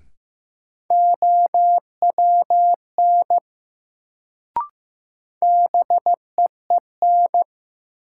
Morse code, heard as a keyed tone, about 15 wpm, 700 Hz. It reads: OWN BEEN
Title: Sets of 2 Words - New Word Drill - 15wpm